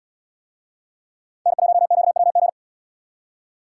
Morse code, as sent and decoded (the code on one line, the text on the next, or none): -. .---- --.- .-- --.
N1QWG